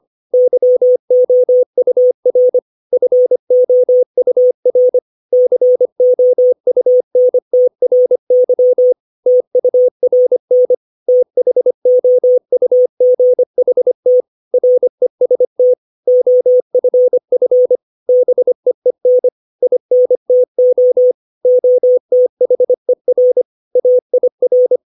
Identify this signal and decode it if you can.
Morse code, heard as a keyed tone, about 25 wpm, 500 Hz. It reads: YOUR FOUR COUNTRY TURN THOUGHT REST OFF BEEN INTO OTHER AIR
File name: 632 words / 632 Words 25WPM